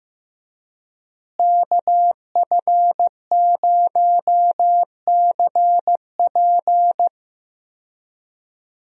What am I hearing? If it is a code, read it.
Morse code, heard as a keyed tone, about 15 wpm, 700 Hz. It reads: KF0CP
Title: USA Call Signs - 5 Characters - Repeated 3x - 15wpm